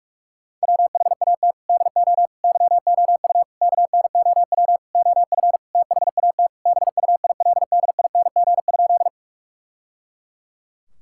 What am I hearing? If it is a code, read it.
Morse code, heard as a keyed tone, about 41 wpm, 700 Hz. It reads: WHAT DO YOU KNOW OF THAT BUILDING?
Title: Sentences from Top 1000 Words - Review Part 5 - 45wpm